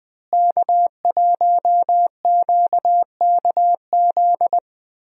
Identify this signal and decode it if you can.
Morse code, heard as a keyed tone, about 20 wpm, 700 Hz. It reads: K1QKZ
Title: USA Call Signs - 5 Characters - Repeated 2x - 20wpm